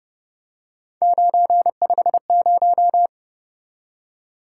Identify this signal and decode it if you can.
Morse code, heard as a keyed tone, about 28 wpm, 700 Hz. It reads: 950